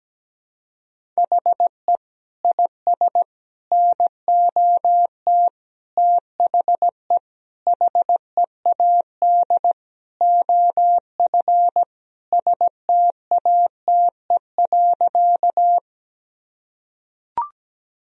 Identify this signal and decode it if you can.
Morse code, heard as a keyed tone, about 17 wpm, 700 Hz. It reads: HE IS NOT THE HEAD OF STATE.
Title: Sentences from Top 1000 Words - Review Part 2 - 17wpm